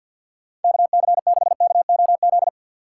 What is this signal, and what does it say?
Morse code, heard as a keyed tone, about 64 wpm, 700 Hz. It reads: KC6KOZ